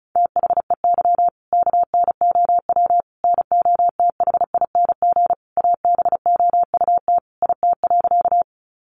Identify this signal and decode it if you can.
Morse code, heard as a keyed tone, about 34 wpm, 700 Hz. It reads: THEY KNOW NOTHING ABOUT IT.
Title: Sentences from Top 500 Words 35wpm